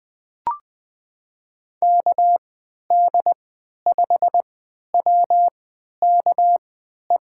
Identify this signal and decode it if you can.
Morse code, heard as a keyed tone, about 20 wpm, spaced as for 11 wpm, 700 Hz. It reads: KD5WKE
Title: USA Call Signs - 3x Character Spacing - 20wpm